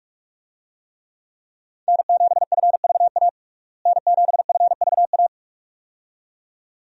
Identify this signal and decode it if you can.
Morse code, heard as a keyed tone, about 46 wpm, 700 Hz. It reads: N7FVA N7FVA
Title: USA Call Signs - 5 Characters - Repeated 2x - 45wpm